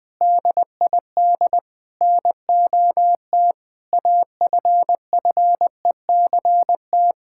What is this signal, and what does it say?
Morse code, heard as a keyed tone, about 20 wpm, 700 Hz. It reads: DID NOT AFFECT
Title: Sentences from Top 1000 Words - Review Part 6 - 20wpm